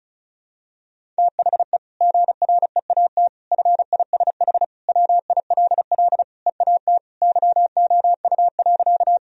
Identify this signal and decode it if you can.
Morse code, heard as a keyed tone, about 36 wpm, 700 Hz. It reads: THE GREAT FISH WILL EAT YOU.
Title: Sentences from Top 1000 Words - Review Part 1 - 35wpm